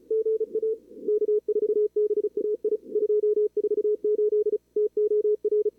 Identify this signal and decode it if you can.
Morse code, heard as a keyed tone, about 34 wpm, 440 Hz, with radio noise. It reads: GA K4BAI 248 TOR
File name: CWSprint / Feb2022 / k5tr-feb-na-sprint-cw-k5ot -0100